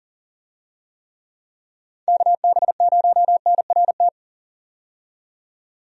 Morse code, heard as a keyed tone, about 40 wpm, 700 Hz. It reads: KB0NRT
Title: USA Call Signs - 6 Characters - 40wpm